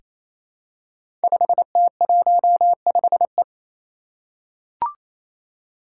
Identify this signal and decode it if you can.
Morse code, heard as a keyed tone, about 27 wpm, 700 Hz. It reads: 5T15E
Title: Five Challenging Letter-Number Combinations 28wpm